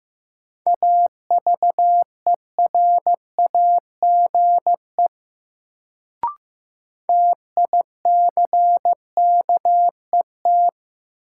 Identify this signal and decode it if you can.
Morse code, heard as a keyed tone, about 15 wpm, 700 Hz. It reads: AVERAGE TICKET
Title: Sentences from Top 1400 Words - New Word Drill Alt - 15wpm